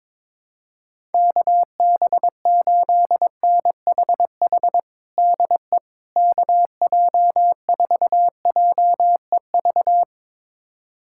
Morse code, heard as a keyed tone, about 22 wpm, 700 Hz. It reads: KB8NHH DE KJ4JEV
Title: USA Callsign DE Callsign 22wpm